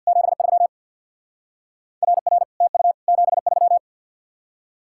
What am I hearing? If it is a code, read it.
Morse code, heard as a keyed tone, about 64 wpm, 700 Hz. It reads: 73 RR TU 73